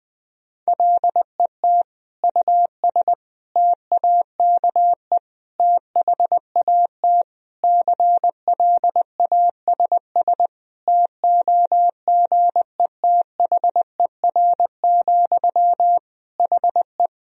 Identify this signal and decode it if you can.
Morse code, heard as a keyed tone, about 20 wpm, 700 Hz. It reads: LET US TAKE THAT CLASS TOGETHER, HE
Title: Sentences from Top 1000 Words - Review Part 2 - 20wpm